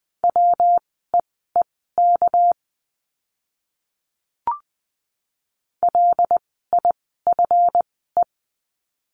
Morse code, heard as a keyed tone, about 20 wpm, spaced as for 14 wpm, 700 Hz. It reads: WEEK LIFE
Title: 230 Lesson 37f - Letter x - Words 20wpm